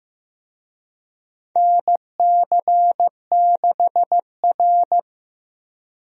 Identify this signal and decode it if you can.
Morse code, heard as a keyed tone, about 15 wpm, 700 Hz. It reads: NC6R
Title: USA Call Signs - 4 Characters - Repeated 3x - 15wpm